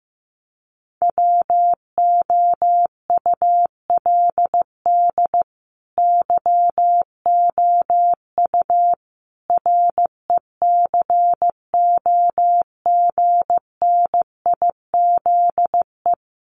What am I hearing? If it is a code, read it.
Morse code, heard as a keyed tone, about 15 wpm, 700 Hz. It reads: WOULD YOU RECOGNIZE
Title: Sentences from Top 1400 Words 15wpm